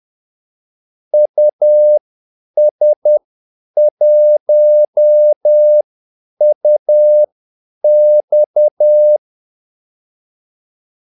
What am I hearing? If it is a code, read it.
Morse code, heard as a keyed tone, about 10 wpm, spaced as for 8 wpm, 600 Hz. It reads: US1UX
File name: Callsigns-10-8wpm